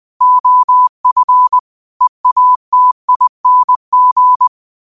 Morse code, heard as a keyed tone, about 20 wpm, 1000 Hz. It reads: OF EATING